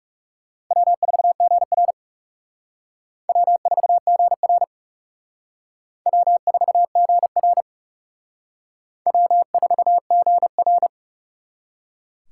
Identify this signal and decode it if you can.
Morse code, heard as a keyed tone, about 37 wpm, 700 Hz. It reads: W4GR W4GR W4GR W4GR
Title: Speed Racing - USA 1x2 Call Signs 30wpm